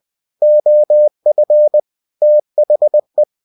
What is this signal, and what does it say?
Morse code, heard as a keyed tone, about 20 wpm, 600 Hz. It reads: OF THE
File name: cw-books / winnie-the-pooh / winnie-20wpm-page-0120